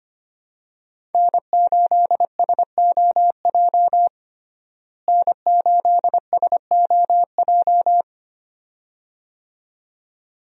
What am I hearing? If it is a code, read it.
Morse code, heard as a keyed tone, about 25 wpm, 700 Hz. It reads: N8SOJ N8SOJ
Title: USA Call Signs - 5 Characters - Repeated 2x - 25wpm